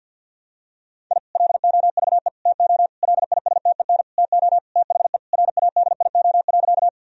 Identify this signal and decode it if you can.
Morse code, heard as a keyed tone, about 51 wpm, 700 Hz. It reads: I LOVE TO LISTEN TO THE RADIO.
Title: Sentences from Top 1000 Words - Review Part 3 - 50wpm